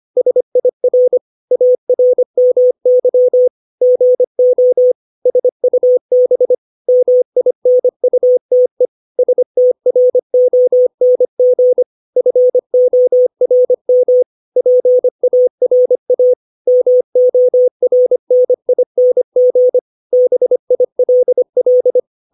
Morse code, heard as a keyed tone, about 25 wpm, 500 Hz. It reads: SIR ARMY GO SUB MINUTE STRONG FORM PARA MORNING BILL
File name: 632 words / 632 Words 25WPM